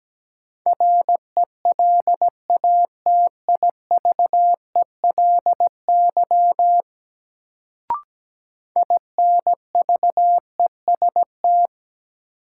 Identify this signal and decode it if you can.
Morse code, heard as a keyed tone, about 17 wpm, 700 Hz. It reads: RELATIVELY INVEST
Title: Sentences from Top 1100 Words - New Word Drill Alt - 17wpm